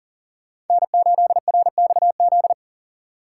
Morse code, heard as a keyed tone, about 49 wpm, 700 Hz. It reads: N8RXZ
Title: Speed Racing - USA 1x3 Call Signs 40wpm